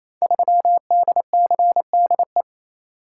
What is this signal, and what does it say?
Morse code, heard as a keyed tone, about 27 wpm, 700 Hz. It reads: 3DCDE
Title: Code Groups - 5 Letter:Numbers - 28wpm